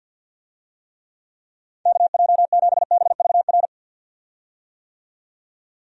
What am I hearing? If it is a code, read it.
Morse code, heard as a keyed tone, about 50 wpm, 700 Hz. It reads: KJ7BVR
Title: USA Call Signs - 6 Characters - 50wpm